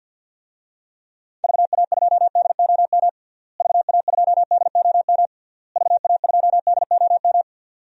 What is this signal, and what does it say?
Morse code, heard as a keyed tone, about 46 wpm, 700 Hz. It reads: VA2DOM VA2DOM VA2DOM
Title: DXCC-100 Call Signs - 6 Characters - Repeated 3x - 50wpm